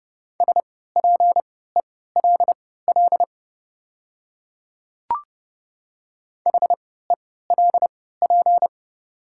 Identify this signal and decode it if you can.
Morse code, heard as a keyed tone, about 30 wpm, spaced as for 17 wpm, 690 Hz. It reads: SPELL HELP
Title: 3x Character Spacing - Top 200 Words - 30wpm